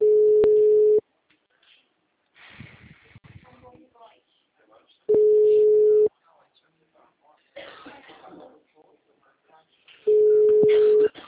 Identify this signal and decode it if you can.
Ringback tone (Europe, 425 Hz), with speech.